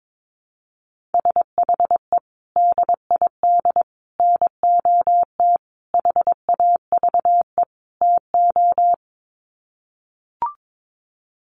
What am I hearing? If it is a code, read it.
Morse code, heard as a keyed tone, about 22 wpm, 700 Hz. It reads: SHE DID NOT HAVE TO
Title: Sets of 5 Words Encore2 22wpm